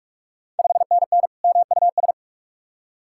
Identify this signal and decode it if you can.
Morse code, heard as a keyed tone, about 44 wpm, 690 Hz. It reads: 5NN MUS
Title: Ontario QSO Party Exchange 45wpm